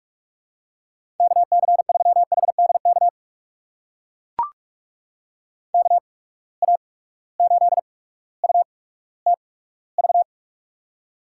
Morse code, heard as a keyed tone, about 44 wpm, spaced as for 19 wpm, 700 Hz. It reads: KC3HDK KA8UTV
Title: USA Call Signs - 6 Characters - 8x Character Spacing - 45wpm